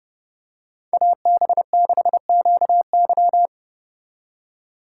Morse code, heard as a keyed tone, about 29 wpm, 700 Hz. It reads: AB6QY